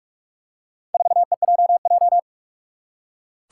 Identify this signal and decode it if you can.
Morse code, heard as a keyed tone, about 41 wpm, 700 Hz. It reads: 4EJJ
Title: Four Challenging Letter-Number Combinations 45wpm